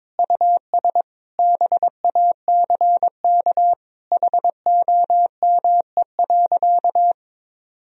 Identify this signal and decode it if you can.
Morse code, heard as a keyed tone, about 22 wpm, 700 Hz. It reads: US BACK HOME.
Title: Sentences from Top 1000 Words - Review Part 2 - 22wpm